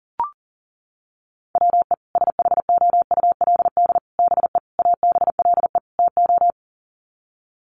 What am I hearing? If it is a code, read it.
Morse code, heard as a keyed tone, about 40 wpm, 700 Hz. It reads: WE SHOULD BE ABLE TO
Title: Sets of 5 Words Encore2 40wpm